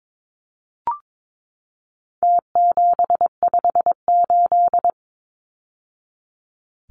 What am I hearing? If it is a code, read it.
Morse code, heard as a keyed tone, about 22 wpm, 700 Hz. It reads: T758
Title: Numbers - Contest Style - 4-Digit 22wpm